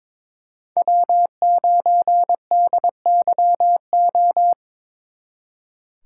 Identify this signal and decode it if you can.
Morse code, heard as a keyed tone, about 26 wpm, 700 Hz. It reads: W9DYO
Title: Speed Racing - USA 1x3 Call Signs 22wpm